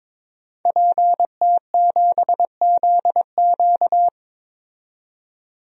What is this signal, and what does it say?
Morse code, heard as a keyed tone, about 22 wpm, 700 Hz. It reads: PT7ZQ
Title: DXCC-100 Call Signs - 5 Characters - 22wpm